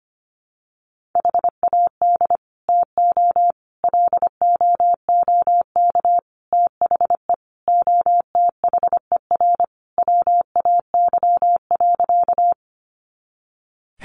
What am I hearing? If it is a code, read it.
Morse code, heard as a keyed tone, about 25 wpm, 700 Hz. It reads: HAD TO LOOK THE OTHER WAY.
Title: Sentences from Top 100 Words 25wpm